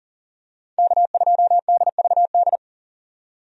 Morse code, heard as a keyed tone, about 40 wpm, 700 Hz. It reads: K2DVD
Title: USA Call Signs - 5 Characters - 40wpm